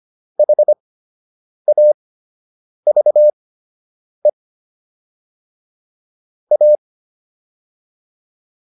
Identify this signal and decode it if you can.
Morse code, heard as a keyed tone, about 25 wpm, spaced as for 8 wpm, 600 Hz. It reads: HAVE A